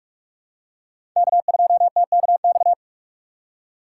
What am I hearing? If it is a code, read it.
Morse code, heard as a keyed tone, about 45 wpm, 700 Hz. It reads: K2TKX
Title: USA Call Signs - 5 Characters - 4x Character Spacing - 45wpm